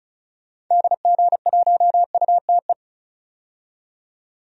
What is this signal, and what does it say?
Morse code, heard as a keyed tone, about 36 wpm, 700 Hz. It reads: DG1UTE
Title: DXCC-100 Call Signs - 6 Characters - 35wpm